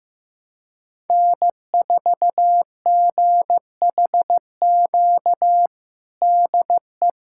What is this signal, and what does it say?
Morse code, heard as a keyed tone, about 16 wpm, 700 Hz. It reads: N4GHQ DE
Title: USA Callsign DE Callsign - 5 Characters - 15wpm